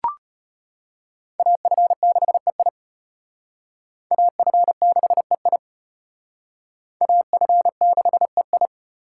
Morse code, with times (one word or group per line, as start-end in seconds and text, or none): 1.4-2.7 AF6EI
4.1-5.6 AF6EI
7.0-8.7 AF6EI